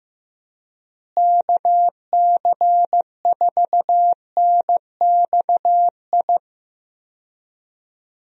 Morse code, heard as a keyed tone, about 16 wpm, 700 Hz. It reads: KC4NXI
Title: USA Call Signs - 6 Characters - Repeated 2x - 15wpm